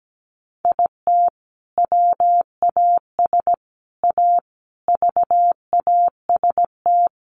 Morse code, heard as a keyed tone, about 17 wpm, 700 Hz. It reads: IT WAS A VAST
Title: Sentences from Top 1300 Words 17wpm